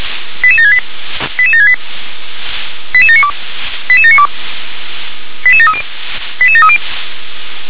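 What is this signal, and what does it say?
Five-tone selective call (ZVEI), heard as quick runs of five tones, tones of about 70 ms. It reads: 88768 88768 88792 88792 88730 88730
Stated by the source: ZVEI five-tone call 88655 received on SDR.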